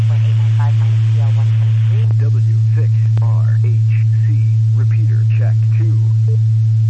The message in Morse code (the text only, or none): none